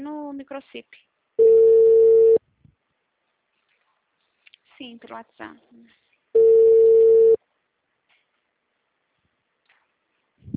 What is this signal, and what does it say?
Ringback tone (China, 450 Hz), with speech.